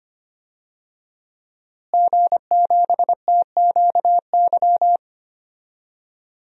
Morse code, heard as a keyed tone, about 25 wpm, 700 Hz. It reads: G7TQY
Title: DXCC-100 Call Signs - 5 Characters - Repeated 3x - 25wpm